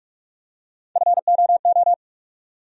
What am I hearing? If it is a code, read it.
Morse code, heard as a keyed tone, about 46 wpm, 700 Hz. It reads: FOO
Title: USA Call Sign Postfixes - 3 Letters 45wpm